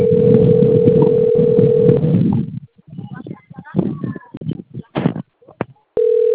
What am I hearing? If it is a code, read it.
Ringback tone (North America), 440 + 480 Hz, with speech.
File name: rg-1600-unknown- - - .50175